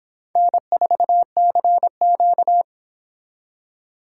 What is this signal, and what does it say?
Morse code, heard as a keyed tone, about 25 wpm, 700 Hz. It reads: N4CQ